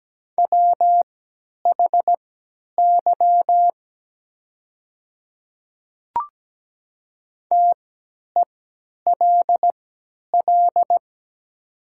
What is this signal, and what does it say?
Morse code, heard as a keyed tone, about 17 wpm, spaced as for 10 wpm, 690 Hz. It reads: WHY TELL
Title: 3x Character Spacing - Top 500 Words - 17wpm